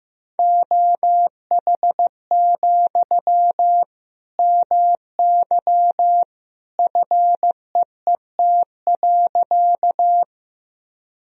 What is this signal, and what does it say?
Morse code, heard as a keyed tone, about 15 wpm, 700 Hz. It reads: OH, MY FEET.
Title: Sentences from Top 1000 Words - Review Part 1 - 15wpm